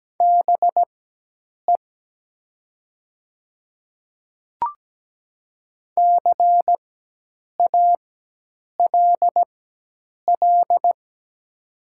Morse code, heard as a keyed tone, about 17 wpm, spaced as for 8 wpm, 700 Hz. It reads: BE CALL
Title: 4x Character Spacing - Top 100 Words - 17wpm